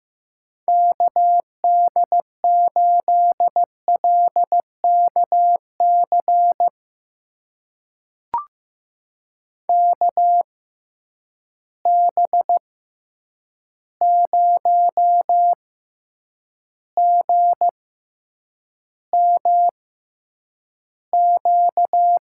Morse code, heard as a keyed tone, about 15 wpm, spaced as for 8 wpm, 700 Hz. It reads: KD8LKC KB0GMQ
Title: USA Call Signs - 6 Characters - 6x Character Spacing - 15wpm